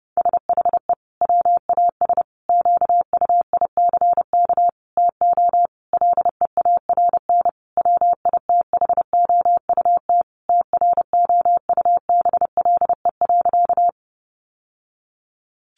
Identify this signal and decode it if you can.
Morse code, heard as a keyed tone, about 30 wpm, 700 Hz. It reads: SHE WAS QUICK TO LEARN WITHOUT TROUBLE.
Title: Sentences from Top 1200 Words 30wpm